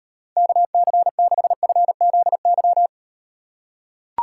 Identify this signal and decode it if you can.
Morse code, heard as a keyed tone, about 38 wpm, 700 Hz. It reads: KC6FZY